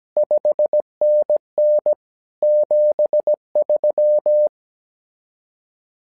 Morse code, heard as a keyed tone, about 17 wpm, 600 Hz. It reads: 5NN 73
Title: SOTA QSO Practice 17wpm